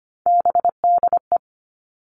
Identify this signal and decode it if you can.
Morse code, heard as a keyed tone, about 25 wpm, 700 Hz. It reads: BDE
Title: All Cut Numbers - 3-Digit 25wpm